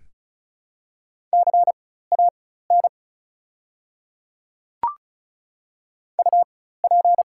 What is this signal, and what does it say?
Morse code, heard as a keyed tone, about 34 wpm, spaced as for 16 wpm, 700 Hz. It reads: CAN UP
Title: 4x Character Spacing - Top 100 Words - 35wpm